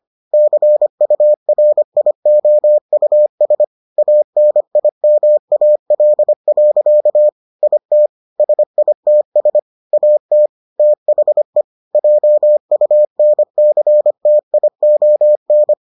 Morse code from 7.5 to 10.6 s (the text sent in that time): IT SITS AT